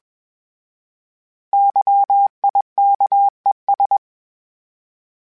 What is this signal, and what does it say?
Morse code, heard as a keyed tone, about 21 wpm, 800 Hz. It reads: YIKES
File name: Misc / Misc-mp3s / Misc @20